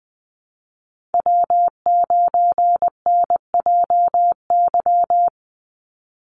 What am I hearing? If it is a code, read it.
Morse code, heard as a keyed tone, about 20 wpm, 700 Hz. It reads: W9NJY
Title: Call Signs - 20wpm